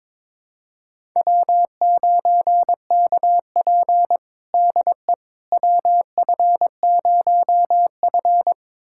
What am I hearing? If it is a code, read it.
Morse code, heard as a keyed tone, about 22 wpm, 700 Hz. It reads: W9KP DE WF0F
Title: USA Callsign DE Callsign - 4 Characters - 22wpm